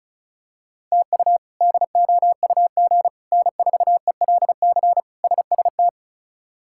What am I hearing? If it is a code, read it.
Morse code, heard as a keyed tone, about 34 wpm, 700 Hz. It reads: TU DOUG N4ELC SST
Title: K1USN SST QSO Practice - 35wpm